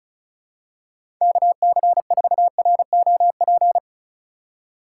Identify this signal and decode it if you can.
Morse code, heard as a keyed tone, about 36 wpm, 700 Hz. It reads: KC4ROP
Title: USA Call Signs - 6 Characters - 2x Character Spacing - 35wpm